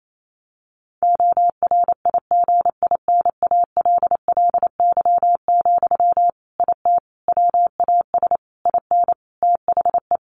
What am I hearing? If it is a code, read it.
Morse code, heard as a keyed tone, about 28 wpm, 700 Hz. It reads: ORIGINALLY, IT WAS IN THE
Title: Sentences from Top 1500 Words 28wpm